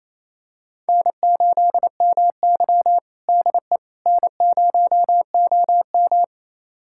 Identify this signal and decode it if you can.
Morse code, heard as a keyed tone, about 28 wpm, 700 Hz. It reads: N8MY DE N0OM
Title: USA Callsign DE Callsign - 4 Characters - 28wpm